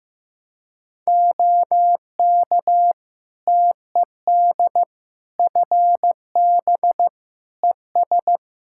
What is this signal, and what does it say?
Morse code, heard as a keyed tone, about 15 wpm, 700 Hz. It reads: OK TED FB ES
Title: Practice QSO - Exchange 2 - Info - 15wpm